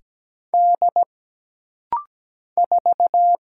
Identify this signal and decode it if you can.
Morse code, heard as a keyed tone, about 17 wpm, 700 Hz. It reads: D 4
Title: Single Letter-Number - Rapid-Fire 17wpm